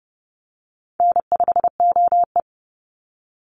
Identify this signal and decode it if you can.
Morse code, heard as a keyed tone, about 30 wpm, 700 Hz. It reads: N5OE
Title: Call Signs - 30wpm